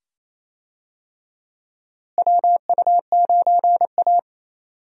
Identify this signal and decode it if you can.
Morse code, heard as a keyed tone, about 28 wpm, 700 Hz. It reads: WU9A
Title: USA Call Signs - 4 Characters - Repeated 2x - 28wpm